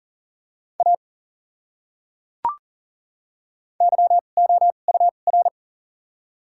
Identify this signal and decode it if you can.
Morse code, heard as a keyed tone, about 40 wpm, spaced as for 29 wpm, 700 Hz. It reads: A YOUR
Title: 2x Character Spacing - Top 100 Words - 40wpm